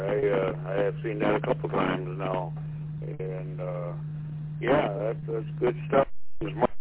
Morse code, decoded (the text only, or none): none